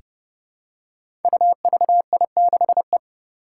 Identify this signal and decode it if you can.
Morse code, heard as a keyed tone, about 30 wpm, 700 Hz. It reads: UVI6E
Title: Code Groups - 5 Letter:Numbers - 30wpm